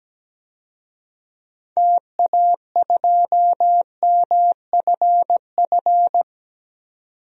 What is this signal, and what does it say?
Morse code, heard as a keyed tone, about 17 wpm, 700 Hz. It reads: TA2MFF
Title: DXCC-100 Call Signs - 6 Characters - 17wpm